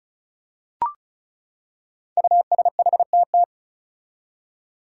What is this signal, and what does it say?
Morse code, heard as a keyed tone, about 36 wpm, 700 Hz. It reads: USHTT
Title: Five Challenging Letter-Number Combinations 35wpm